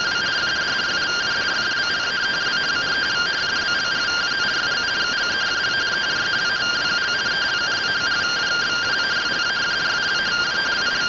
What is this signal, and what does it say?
Two-tone teleprinter signal, about 170 Hz shift.
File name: findings / MR CORONAL fr1 -131.gen-15 rtty rx